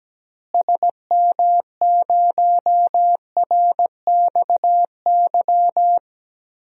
Morse code, heard as a keyed tone, about 17 wpm, 700 Hz. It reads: SM0RXY
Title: DXCC-100 Call Signs - 6 Characters - Repeated 3x - 17wpm